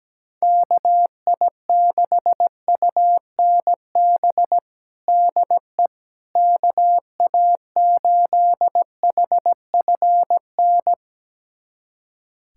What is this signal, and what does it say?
Morse code, heard as a keyed tone, about 17 wpm, 700 Hz. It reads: KI6UNB DE KA8HFN
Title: USA Callsign DE Callsign - 6 Characters - 17wpm